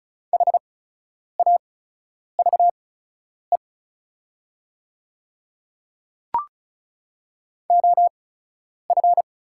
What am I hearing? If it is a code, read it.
Morse code, heard as a keyed tone, about 36 wpm, spaced as for 10 wpm, 700 Hz. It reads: HAVE OF
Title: 8x Character Spacing - Top 100 Words - 35wpm